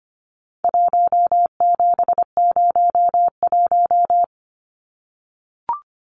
Morse code, heard as a keyed tone, about 25 wpm, 700 Hz. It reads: 1701
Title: Numbers - 4-Digit 25wpm